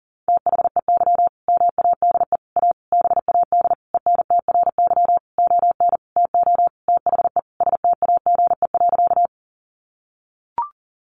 Morse code, heard as a keyed tone, about 40 wpm, 700 Hz. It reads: THEY MADE A BAD ENTRY ON TO THE STAGE.
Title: Sentences from Top 1500 Words 40wpm